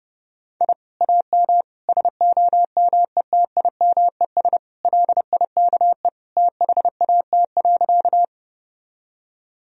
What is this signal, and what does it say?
Morse code, heard as a keyed tone, about 29 wpm, 700 Hz. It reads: I AM SOMETIMES LIKE THAT.
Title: Sentences from Top 1000 Words - Review Part 3 - 30wpm